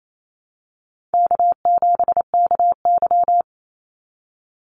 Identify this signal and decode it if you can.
Morse code, heard as a keyed tone, about 28 wpm, 700 Hz. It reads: K7KY
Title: Easy Call Signs - 28wpm